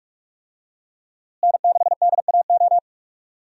Morse code, heard as a keyed tone, about 42 wpm, 700 Hz. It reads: N6DAO
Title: USA Call Signs - 5 Characters - Repeated 2x - 45wpm